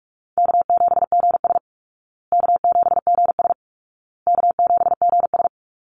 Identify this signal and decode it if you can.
Morse code, heard as a keyed tone, about 44 wpm, 700 Hz. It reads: K7GS K7GS K7GS
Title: Easy Call Signs Repeated 3x - 45wpm